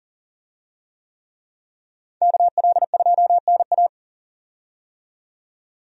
Morse code, heard as a keyed tone, about 40 wpm, 700 Hz. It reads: KL2NA
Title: USA Call Signs - 5 Characters - 40wpm